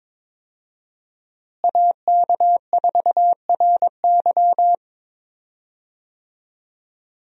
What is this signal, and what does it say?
Morse code, heard as a keyed tone, about 22 wpm, 700 Hz. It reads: AK4RY
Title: USA Call Signs - 5 Characters - Repeated 3x - 22wpm